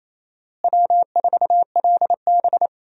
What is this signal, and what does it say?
Morse code, heard as a keyed tone, about 28 wpm, 700 Hz. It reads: W4LB
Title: USA Call Signs - 4 Characters - Repeated 2x - 28wpm